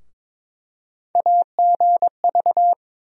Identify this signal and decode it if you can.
Morse code, heard as a keyed tone, about 22 wpm, 700 Hz. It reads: AGV
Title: USA Call Sign Postfixes - 3 Letters 22wpm